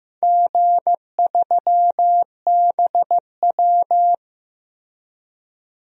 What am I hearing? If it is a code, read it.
Morse code, heard as a keyed tone, about 15 wpm, 700 Hz. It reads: G3BW
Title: DXCC-100 Call Signs - 4 Characters - 15wpm